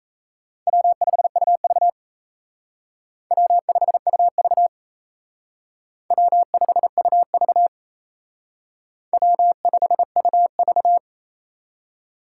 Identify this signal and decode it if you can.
Morse code, heard as a keyed tone, about 35 wpm, 700 Hz. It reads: W5UV W5UV W5UV W5UV
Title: Speed Racing - USA 1x2 Call Signs 28wpm